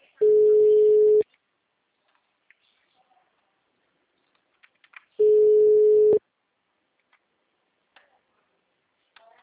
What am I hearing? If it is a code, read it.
Ringback tone (Europe, 425 Hz).
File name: out- -2009- - - .1695